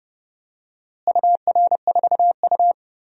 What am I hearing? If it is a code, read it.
Morse code, heard as a keyed tone, about 30 wpm, 700 Hz. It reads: UR4U
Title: DXCC-100 Call Signs - 4 Characters - Repeated 2x - 30wpm